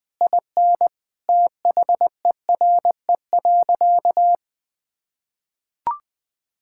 Morse code, heard as a keyed tone, about 21 wpm, 700 Hz. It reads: IN THERE.
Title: Sentences with 3-Word Phrases - 20wpm